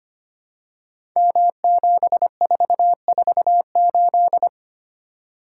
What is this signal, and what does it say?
Morse code, heard as a keyed tone, about 25 wpm, 700 Hz. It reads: M7448
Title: Code Groups - 5 Letter:Numbers - 25wpm